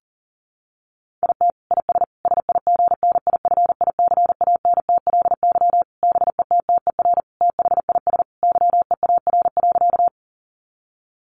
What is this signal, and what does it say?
Morse code, heard as a keyed tone, about 40 wpm, 700 Hz. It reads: IT IS SIGNIFICANTLY BETTER THIS YEAR.
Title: Sentences from Top 1500 Words 40wpm